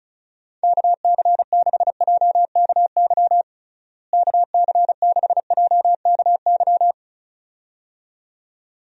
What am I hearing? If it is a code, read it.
Morse code, heard as a keyed tone, about 36 wpm, 700 Hz. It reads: KC6JKY KC6JKY
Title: USA Call Signs - 6 Characters - Repeated 2x - 35wpm